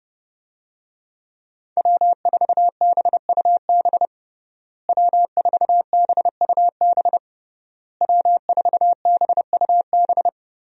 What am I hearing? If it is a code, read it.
Morse code, heard as a keyed tone, about 32 wpm, 700 Hz. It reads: W4BUB W4BUB W4BUB
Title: USA Call Signs - 5 Characters - Repeated 3x - 30wpm